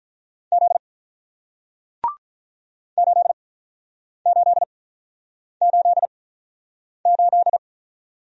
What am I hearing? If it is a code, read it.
Morse code, heard as a keyed tone, about 48 wpm, 700 Hz. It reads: Z 8 8 8 8